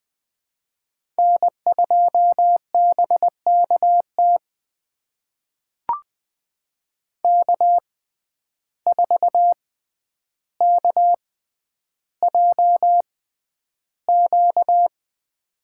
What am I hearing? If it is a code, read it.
Morse code, heard as a keyed tone, about 20 wpm, spaced as for 10 wpm, 700 Hz. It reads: N2BKT K4KJQ
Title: USA Call Signs - 5 Characters - 6x Character Spacing - 20wpm